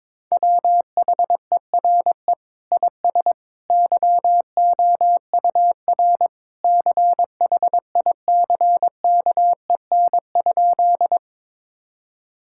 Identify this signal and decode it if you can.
Morse code, heard as a keyed tone, about 22 wpm, 700 Hz. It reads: WHERE IS YOUR CHICKEN?
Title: Sentences from Top 1000 Words - Review Part 5 - 22wpm